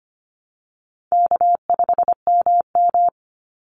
Morse code, heard as a keyed tone, about 25 wpm, 700 Hz. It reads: K5MM
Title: Call Signs Repeated 2x - 25wpm